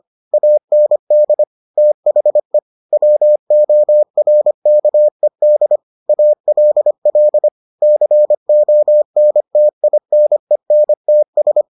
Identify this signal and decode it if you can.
Morse code, heard as a keyed tone, about 25 wpm, 600 Hz. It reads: AND THE WORKED ALL CONTINENTS